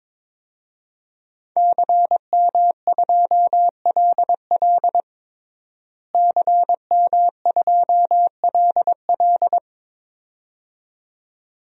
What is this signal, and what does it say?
Morse code, heard as a keyed tone, about 21 wpm, 700 Hz. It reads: CM2LL CM2LL
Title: DXCC-100 Call Signs - 5 Characters - Repeated 2x - 22wpm